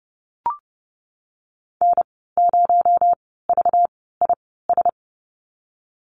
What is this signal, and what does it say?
Morse code, heard as a keyed tone, about 30 wpm, spaced as for 17 wpm, 700 Hz. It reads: N0VIS
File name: Lesson 37g - Letter x - Call Signs 30wpm